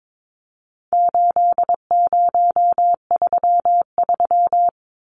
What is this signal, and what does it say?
Morse code, heard as a keyed tone, about 22 wpm, 700 Hz. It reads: 8033
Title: Numbers - 4-Digit 22wpm